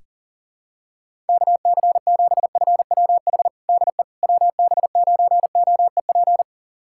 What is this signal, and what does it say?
Morse code, heard as a keyed tone, about 40 wpm, 700 Hz. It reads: KC7FWH DE WB9OEP
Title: USA Callsign DE Callsign - 6 Characters - 40wpm